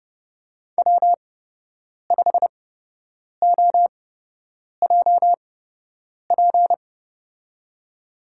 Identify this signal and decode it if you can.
Morse code, heard as a keyed tone, about 30 wpm, spaced as for 8 wpm, 700 Hz. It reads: W5OJP
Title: USA Call Signs - 8x Character Spacing - 30wpm